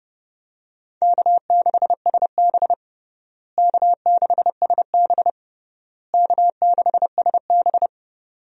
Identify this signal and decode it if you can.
Morse code, heard as a keyed tone, about 30 wpm, 700 Hz. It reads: K6SB K6SB K6SB
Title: USA Call Signs - 4 Characters - Repeated 3x - 30wpm